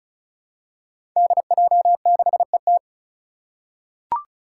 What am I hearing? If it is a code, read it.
Morse code, heard as a keyed tone, about 36 wpm, 700 Hz. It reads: DJ6ET
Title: Five Challenging Letter-Number Combinations 35wpm